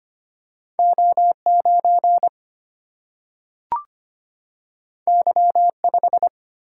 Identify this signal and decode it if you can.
Morse code, heard as a keyed tone, about 25 wpm, 700 Hz. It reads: O9 Y5
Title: Single-Letter Single-Number 25wpm